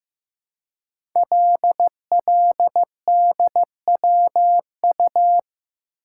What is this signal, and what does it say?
Morse code, heard as a keyed tone, about 15 wpm, 700 Hz. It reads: LLDWU
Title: Code Groups - 5 Letter:Numbers - 15wpm